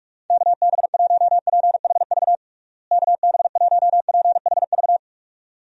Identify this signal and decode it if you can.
Morse code, heard as a keyed tone, about 46 wpm, 700 Hz. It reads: KB1PHV KB1PHV
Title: USA Call Signs - 6 Characters - Repeated 2x - 45wpm